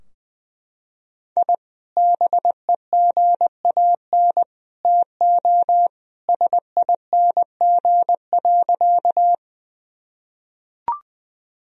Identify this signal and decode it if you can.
Morse code, heard as a keyed tone, about 20 wpm, 700 Hz. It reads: I BEGAN TO SING.
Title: Sentences from Top 1000 Words - Review Part 2 - 20wpm